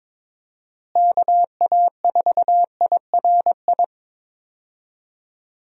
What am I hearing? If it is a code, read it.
Morse code, heard as a keyed tone, about 22 wpm, 700 Hz. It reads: KA4IRI
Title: USA Call Signs - 6 Characters - Repeated 2x - 22wpm